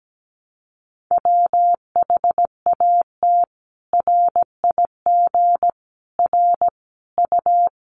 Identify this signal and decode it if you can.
Morse code, heard as a keyed tone, about 17 wpm, 700 Hz. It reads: WHAT RIG R U
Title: Common QSO Phrases 17wpm